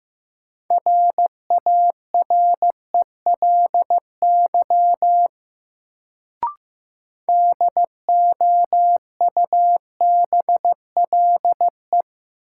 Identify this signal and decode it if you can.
Morse code, heard as a keyed tone, about 15 wpm, 700 Hz. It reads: RARELY DOUBLE
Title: Sentences from Top 1400 Words - New Word Drill Alt - 15wpm